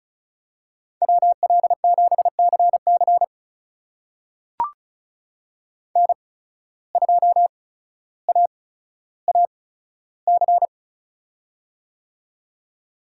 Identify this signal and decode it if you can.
Morse code, heard as a keyed tone, about 34 wpm, spaced as for 15 wpm, 700 Hz. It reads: WL7CC N2AAC
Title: USA Call Signs - 5 Characters - 8x Character Spacing - 35wpm